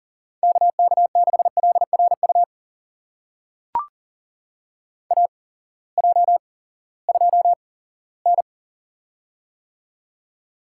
Morse code, heard as a keyed tone, about 40 wpm, spaced as for 20 wpm, 700 Hz. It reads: KK6LRU AJ2N
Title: USA Call Signs - 8x Character Spacing - 40wpm